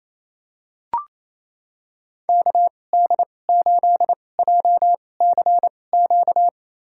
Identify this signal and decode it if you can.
Morse code, heard as a keyed tone, about 28 wpm, spaced as for 20 wpm, 700 Hz. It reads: KD8JCQ
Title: USA Call Signs - 6 Characters - 2x Character Spacing - 28wpm